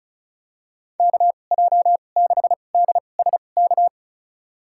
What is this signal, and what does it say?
Morse code, heard as a keyed tone, about 36 wpm, spaced as for 25 wpm, 700 Hz. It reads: KJ6DSK
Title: USA Call Signs - 2x Character Spacing - 35wpm